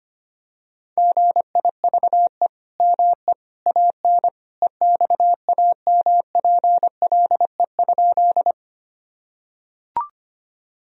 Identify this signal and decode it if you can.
Morse code, heard as a keyed tone, about 25 wpm, 700 Hz. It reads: GIVE ME AN EXAMPLE?
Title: Sentences from Top 1000 Words - Review Part 1 - 25wpm